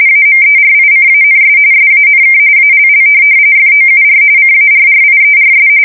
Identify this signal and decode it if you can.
RTTY, about 45.45 baud. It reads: AMERICANS SANK ALL FOUR CARRIERS